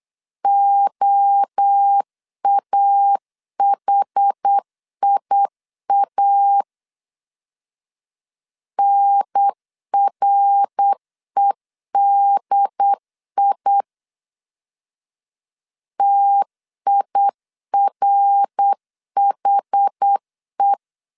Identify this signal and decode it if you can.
Morse code, heard as a keyed tone, about 8 wpm, 800 Hz. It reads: OAHIA NREDI TIRHE
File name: CW / DET / 1 3 1500C S B 8wpm 4141